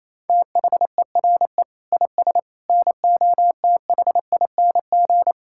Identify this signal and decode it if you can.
Morse code, heard as a keyed tone, about 28 wpm, 700 Hz. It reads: THERE IS NOTHING
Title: Sentences from Top 1600 Words 28wpm